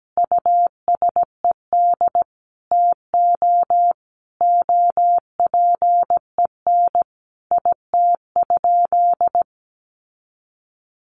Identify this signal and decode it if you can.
Morse code, heard as a keyed tone, about 17 wpm, 700 Hz. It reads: USED TO OPEN IT?
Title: Sentences from Top 600 Words 17wpm